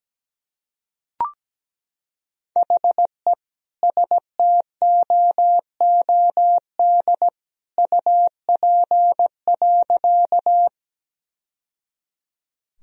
Morse code, heard as a keyed tone, about 17 wpm, 700 Hz. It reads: HE STOOD UP.
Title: Sentences from Top 1000 Words - Review Part 2 - 17wpm